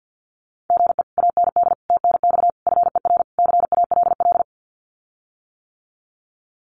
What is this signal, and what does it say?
Morse code, heard as a keyed tone, about 50 wpm, 700 Hz. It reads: GE AND TNX FER CALL
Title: Common QSO Phrases Encore 50wpm